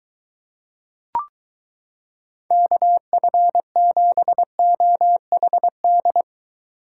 Morse code, heard as a keyed tone, about 22 wpm, 700 Hz. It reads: KF7OHD